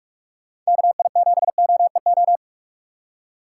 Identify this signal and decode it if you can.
Morse code, heard as a keyed tone, about 43 wpm, 700 Hz. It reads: KI7OEO
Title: USA Call Signs - 6 Characters - 4x Character Spacing - 45wpm